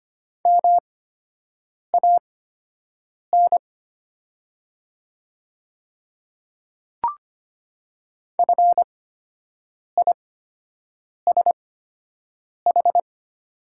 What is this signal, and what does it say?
Morse code, heard as a keyed tone, about 25 wpm, spaced as for 7 wpm, 700 Hz. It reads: MAN FISH
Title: 8x Character Spacing - Top 300 Words - 25wpm